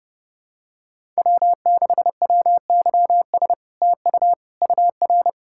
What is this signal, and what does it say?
Morse code, heard as a keyed tone, about 30 wpm, 700 Hz. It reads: W6WYS TU UR
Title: SOTA QSO Practice 30wpm